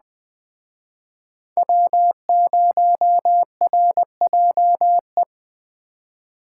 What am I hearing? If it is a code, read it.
Morse code, heard as a keyed tone, about 20 wpm, 700 Hz. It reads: W0RJE
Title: Speed Racing - USA Call Signs 17wpm